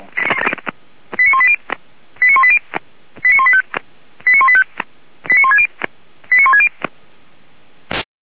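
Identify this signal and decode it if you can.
Five-tone selective call (ZVEI), heard as quick runs of five tones, tones of about 80 ms. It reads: 89180 89180 89185 89185 89160 89160
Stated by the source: ZVEI five-tone selective call received by SDR.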